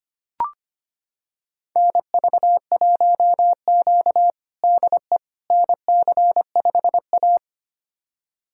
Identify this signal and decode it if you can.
Morse code, heard as a keyed tone, about 25 wpm, 700 Hz. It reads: NV1Q DE NC5A
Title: USA Callsign DE Callsign - 4 Characters - 25wpm